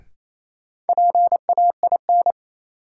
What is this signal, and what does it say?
Morse code, heard as a keyed tone, about 28 wpm, 700 Hz. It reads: PAIN
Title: Sentences from Top 1600 Words - New Word Drill Alt - 28wpm